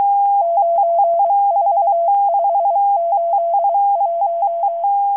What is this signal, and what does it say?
Morse code, heard as a keyed tone, about 23 wpm, spaced as for 21 wpm, 680 Hz. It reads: 94581
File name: HM02- -0437utc-7351khz